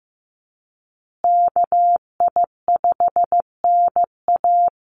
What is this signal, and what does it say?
Morse code, heard as a keyed tone, about 15 wpm, 700 Hz. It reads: KI5NA
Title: USA Call Signs Repeated 3x 15wpm